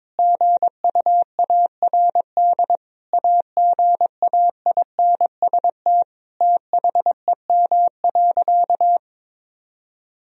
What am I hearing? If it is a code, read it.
Morse code, heard as a keyed tone, about 22 wpm, 700 Hz. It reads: GUARD AGAINST THEM.
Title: Sentences from Top 1000 Words - Review Part 4 - 22wpm